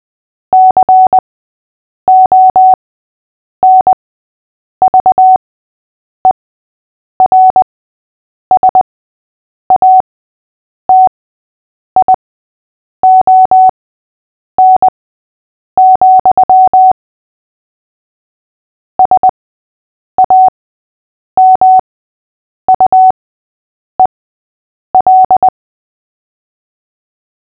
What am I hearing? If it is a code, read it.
Morse code, heard as a keyed tone, about 20 wpm, spaced as for 8 wpm, 750 Hz. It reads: CONVERSATION, SAMUEL
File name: cw / fists / Short-Story-6-8wpm